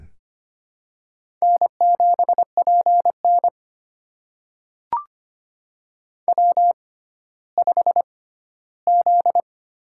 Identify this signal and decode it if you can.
Morse code, heard as a keyed tone, about 25 wpm, spaced as for 14 wpm, 700 Hz. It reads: N7PN W5Z
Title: USA Call Signs - 6x Character Spacing - 25wpm